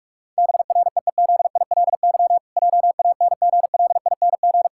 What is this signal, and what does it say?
Morse code, heard as a keyed tone, about 46 wpm, 700 Hz. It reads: BREEZILY JANGLING
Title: Pangram Challenge v2 - 45wpm